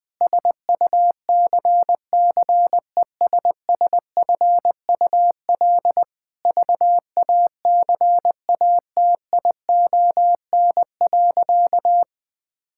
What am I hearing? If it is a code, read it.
Morse code, heard as a keyed tone, about 19 wpm, 700 Hz. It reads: SUCCESSFUL VACATION.
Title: Difficult Sentences with Hints - 20wpm